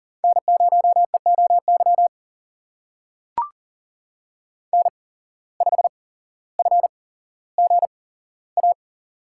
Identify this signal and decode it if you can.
Morse code, heard as a keyed tone, about 40 wpm, spaced as for 17 wpm, 700 Hz. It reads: N0EOY N5FGA
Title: USA Call Signs - 5 Characters - 8x Character Spacing - 40wpm